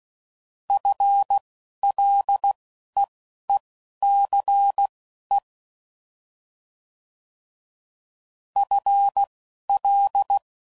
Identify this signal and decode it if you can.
Morse code, heard as a keyed tone, about 16 wpm, spaced as for 11 wpm, 800 Hz. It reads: FLEECE FL
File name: Prefixes / FLs & THs / FL Prefixes 1